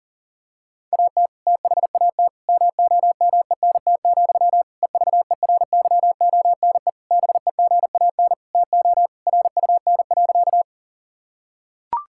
Morse code, heard as a keyed tone, about 40 wpm, 700 Hz. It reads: AT THAT MOMENT, EVERYONE BEGAN TO RUN.
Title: Sentences from Top 1000 Words - Review Part 5 - 40wpm